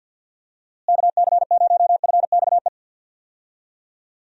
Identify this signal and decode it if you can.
Morse code, heard as a keyed tone, about 51 wpm, 700 Hz. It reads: KC0FXE
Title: USA Call Signs - 6 Characters - Repeated 2x - 50wpm